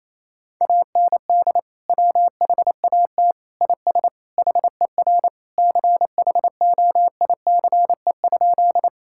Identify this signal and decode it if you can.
Morse code, heard as a keyed tone, about 27 wpm, 700 Hz. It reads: AND WHAT IS HER CHOICE?
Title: Sentences from Top 1000 Words - Review Part 6 - 28wpm